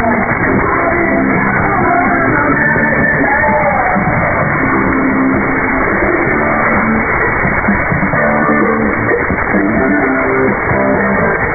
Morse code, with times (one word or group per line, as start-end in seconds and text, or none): none